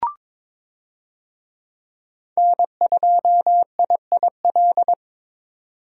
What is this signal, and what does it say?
Morse code, heard as a keyed tone, about 21 wpm, 700 Hz. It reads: N2IIL
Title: USA Call Signs - 5 Characters - 22wpm